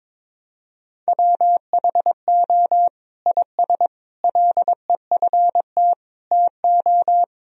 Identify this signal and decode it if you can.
Morse code, heard as a keyed tone, about 22 wpm, 700 Hz. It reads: WHO IS LEFT TO
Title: Sentences from Top 1000 Words - Review Part 2 - 22wpm